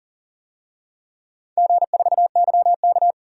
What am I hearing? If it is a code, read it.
Morse code, heard as a keyed tone, about 40 wpm, 700 Hz. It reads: G4YK
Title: DXCC-100 Call Signs - 4 Characters - 40wpm